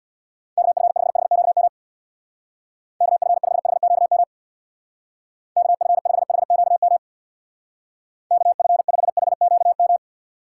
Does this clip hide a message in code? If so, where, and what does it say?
Yes: Morse code from 0.6 to 10.0 s, heard as a keyed tone, about 65 wpm, 700 Hz. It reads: KF5HQM KF5HQM KF5HQM KF5HQM